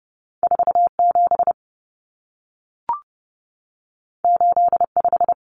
Morse code, heard as a keyed tone, about 30 wpm, 700 Hz. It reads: 47 85
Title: Numbers - 2-Digit 30wpm